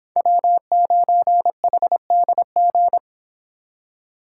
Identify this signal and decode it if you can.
Morse code, heard as a keyed tone, about 26 wpm, 700 Hz. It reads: W9HDG